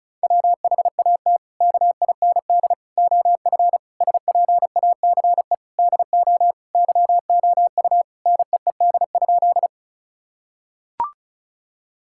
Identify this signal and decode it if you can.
Morse code, heard as a keyed tone, about 34 wpm, 700 Hz. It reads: WHAT KIND OF SPACE DO YOU NEED?
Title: Sentences from Top 1000 Words - Review Part 5 - 35wpm